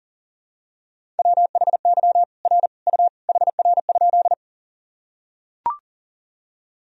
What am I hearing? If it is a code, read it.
Morse code, heard as a keyed tone, about 38 wpm, 700 Hz. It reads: WHY R U HR?
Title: Sentences from Top 200 Words and Abbreviations 40wpm